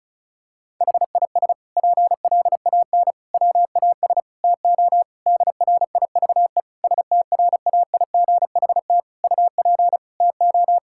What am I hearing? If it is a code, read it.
Morse code, heard as a keyed tone, about 36 wpm, 700 Hz. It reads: HIS PLAN WAS TO DRIVE STRAIGHT UP TO
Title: Sentences from Top 1000 Words - Review Part 6 - 35wpm